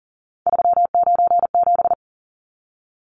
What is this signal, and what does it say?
Morse code, heard as a keyed tone, about 40 wpm, 700 Hz. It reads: 397
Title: Numbers - 3-Digit 40wpm